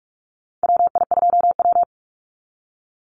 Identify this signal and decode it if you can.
Morse code, heard as a keyed tone, about 46 wpm, 700 Hz. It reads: WI2W
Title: NA SOTA Chaser Callsigns 45wpm